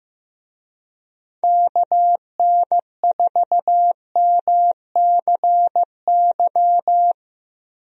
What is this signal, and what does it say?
Morse code, heard as a keyed tone, about 15 wpm, 700 Hz. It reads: KN4MCY
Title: USA Call Signs - 6 Characters - Repeated 2x - 15wpm